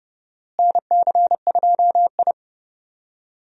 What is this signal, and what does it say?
Morse code, heard as a keyed tone, about 30 wpm, 700 Hz. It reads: NC2I
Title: USA Call Signs - 4 Characters - Repeated 2x - 30wpm